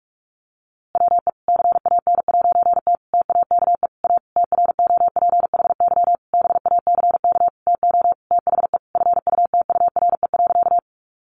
Morse code, heard as a keyed tone, about 44 wpm, 700 Hz. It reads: WE CAN'T TAKE A TROPHY BACK TO THE FUTURE.
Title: A Sound Of Thunder 45wpm